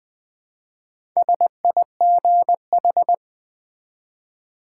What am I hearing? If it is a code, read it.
Morse code, heard as a keyed tone, about 20 wpm, 700 Hz. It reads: SIGH
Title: Build A Word - 1 Letter At A Time - 3 to 5-Letter Words - 20wpm